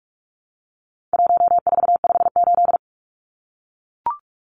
Morse code, heard as a keyed tone, about 44 wpm, 700 Hz. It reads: 1458
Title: Numbers - 4-Digit 45wpm